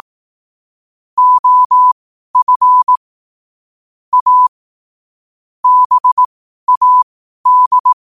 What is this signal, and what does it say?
Morse code, heard as a keyed tone, about 18 wpm, spaced as for 12 wpm, 1000 Hz. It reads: OF A BAD
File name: qotdfiles / 12wpm